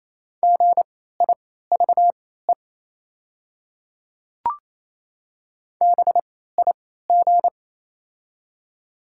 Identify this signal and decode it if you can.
Morse code, heard as a keyed tone, about 28 wpm, spaced as for 16 wpm, 700 Hz. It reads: GIVE BIG
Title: 3x Character Spacing - Top 200 Words - 28wpm